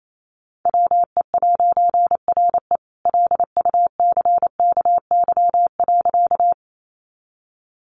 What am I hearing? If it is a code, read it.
Morse code, heard as a keyed tone, about 28 wpm, 700 Hz. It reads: WE'RE LUCKY.
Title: A Sound Of Thunder 28wpm